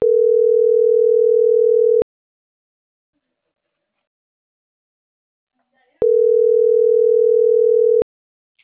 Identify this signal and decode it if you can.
Ringback tone (North America), 440 + 480 Hz.